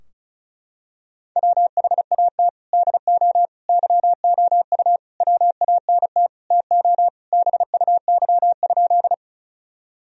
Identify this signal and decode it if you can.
Morse code, heard as a keyed tone, about 36 wpm, 700 Hz. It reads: WHAT DO YOU WANT TO BUY?
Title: Sentences from Top 1000 Words - Review Part 4 - 35wpm